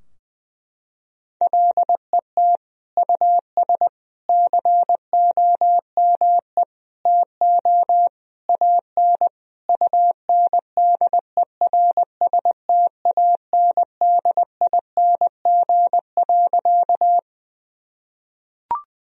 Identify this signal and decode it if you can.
Morse code, heard as a keyed tone, about 21 wpm, 700 Hz. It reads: LET US COME TO AN UNDERSTANDING.
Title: Sentences from Top 1000 Words - Review Part 3 - 20wpm